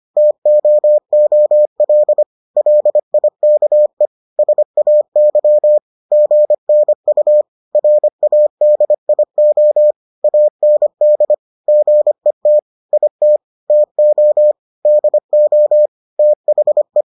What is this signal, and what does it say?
Morse code, heard as a keyed tone, about 25 wpm, 600 Hz. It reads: TOOL LIKE SAY GNU RADIO AND GET IT TO DO THE